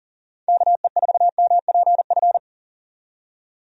Morse code, heard as a keyed tone, about 40 wpm, 700 Hz. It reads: KE4MPF
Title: USA Call Signs - 6 Characters - 40wpm